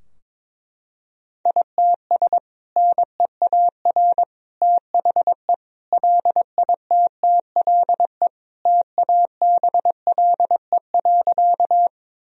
Morse code, heard as a keyed tone, about 22 wpm, 700 Hz. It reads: ITS NEAR THE LITTLE TABLE.
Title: Sentences from Top 1000 Words - Review Part 2 - 22wpm